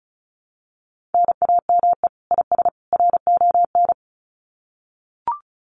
Morse code, heard as a keyed tone, about 34 wpm, 700 Hz. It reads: NAME IS RON
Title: Common QSO Phrases 35wpm